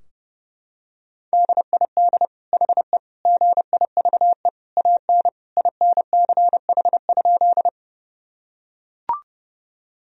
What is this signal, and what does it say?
Morse code, heard as a keyed tone, about 32 wpm, 700 Hz. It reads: DID HE GIVE AN INCH?
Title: Sentences from Top 1000 Words - Review Part 3 - 30wpm